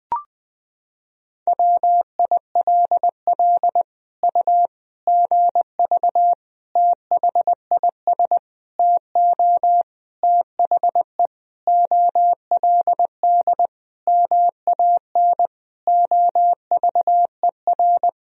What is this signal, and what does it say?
Morse code, heard as a keyed tone, about 21 wpm, 700 Hz. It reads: WILL U GV THIS TO THE OLD MAN OVER
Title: Sentences from Top 200 Words and Abbreviations 20wpm